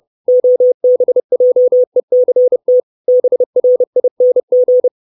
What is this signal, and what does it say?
Morse code, heard as a keyed tone, about 30 wpm, 500 Hz. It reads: OBJECT BRING
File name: 632 Words 30WPM